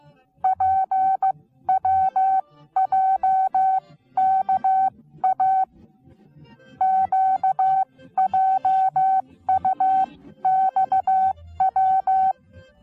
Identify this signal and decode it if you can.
Morse code, heard as a keyed tone, about 15 wpm, spaced as for 13 wpm, 760 Hz. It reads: PWJKA QJUXW
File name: Onlinelehrgang / morsen / DJ4UF Morsen19